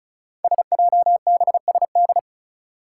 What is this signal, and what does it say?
Morse code, heard as a keyed tone, about 36 wpm, 700 Hz. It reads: SJBSD
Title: Five Challenging Letter-Number Combinations 35wpm